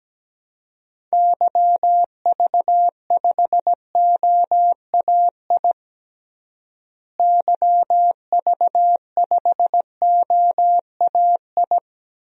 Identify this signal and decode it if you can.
Morse code, heard as a keyed tone, about 17 wpm, 700 Hz. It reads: YV5OAI YV5OAI
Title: DXCC-100 Call Signs - 6 Characters - Repeated 3x - 17wpm